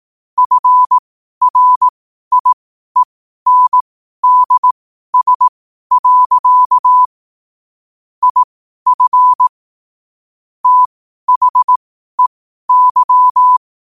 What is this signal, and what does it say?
Morse code, heard as a keyed tone, about 18 wpm, spaced as for 12 wpm, 1000 Hz. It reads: FRIENDS. IF THEY
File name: qotdfiles / 12wpm